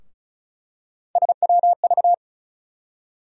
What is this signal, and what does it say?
Morse code, heard as a keyed tone, about 34 wpm, 700 Hz. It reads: SWV
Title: USA Call Sign Postfixes - 3 Letters 35wpm